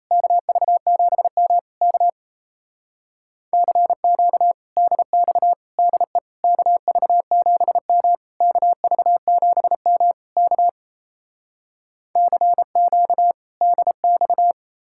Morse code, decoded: KV7M K CQ DX DE KV7M KV7M K CQ DX

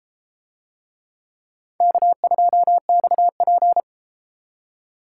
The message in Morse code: K2XP

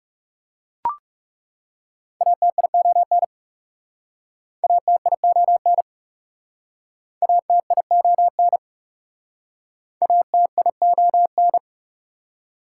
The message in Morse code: ATION ATION ATION ATION

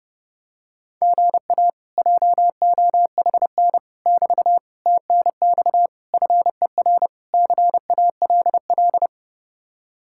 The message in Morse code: GA JOHN <BT> TNX FER CALL

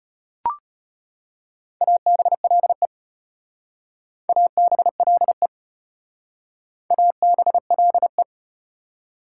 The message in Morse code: ABLE ABLE ABLE

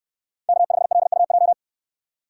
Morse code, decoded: K5LUJ

